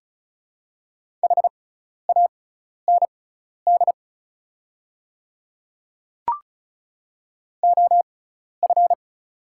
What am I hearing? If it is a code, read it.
Morse code, heard as a keyed tone, about 34 wpm, spaced as for 12 wpm, 700 Hz. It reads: HAND OF
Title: 6x Character Spacing - Top 200 Words - 35wpm